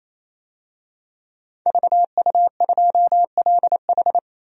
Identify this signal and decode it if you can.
Morse code, heard as a keyed tone, about 28 wpm, 700 Hz. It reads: VU2LH
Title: DXCC-100 Call Signs - 5 Characters - 28wpm